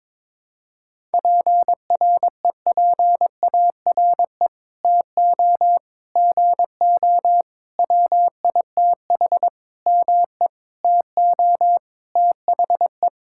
Morse code, heard as a keyed tone, about 22 wpm, 700 Hz. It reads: PREPARE TO GO WITH ME TO THE
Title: Sentences from Top 1000 Words - Review Part 6 - 22wpm